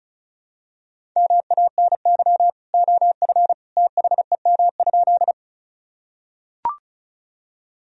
Morse code, heard as a keyed tone, about 36 wpm, 700 Hz. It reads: MANY OF THEM?
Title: Sentences from Top 1000 Words - Review Part 1 - 35wpm